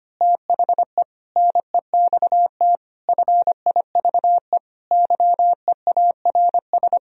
Morse code, heard as a keyed tone, about 24 wpm, 700 Hz. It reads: THE NEXT FIVE YEARS
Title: Speed-Racing - Sets of 5 Words 25wpm